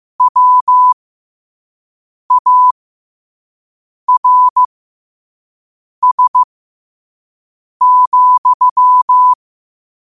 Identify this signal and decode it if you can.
Morse code, heard as a keyed tone, about 15 wpm, spaced as for 5 wpm, 1000 Hz. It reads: WARS,
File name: qotdfiles / 5wpm